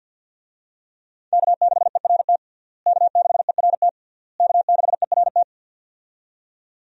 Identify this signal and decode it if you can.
Morse code, heard as a keyed tone, about 50 wpm, 700 Hz. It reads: K6ERT K6ERT K6ERT
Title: USA Call Signs - 5 Characters - Repeated 3x - 50wpm